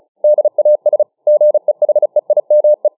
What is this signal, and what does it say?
Morse code, heard as a keyed tone, about 36 wpm, 600 Hz. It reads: DAS GEHEIME